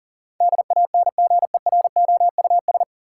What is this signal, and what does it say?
Morse code, heard as a keyed tone, about 39 wpm, 700 Hz. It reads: DANGEROUS
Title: Sentences from Top 1200 Words - New Word Drill - 40wpm